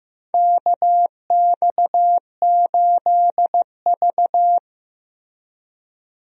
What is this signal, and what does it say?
Morse code, heard as a keyed tone, about 15 wpm, 700 Hz. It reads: KX8V
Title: USA Call Signs - 4 Characters - Repeated 2x - 15wpm